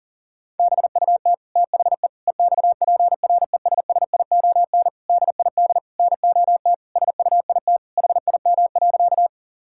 Morse code, heard as a keyed tone, about 37 wpm, 700 Hz. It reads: BUT THE EXPRESSION DID NOT SUIT HIM.
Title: Sentences from Top 1600 Words 40wpm